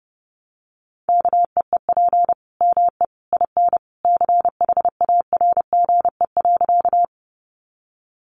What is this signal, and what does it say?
Morse code, heard as a keyed tone, about 30 wpm, 700 Hz. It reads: KEEP ME IN CHARGE.
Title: Sentences from Top 900 Words 30wpm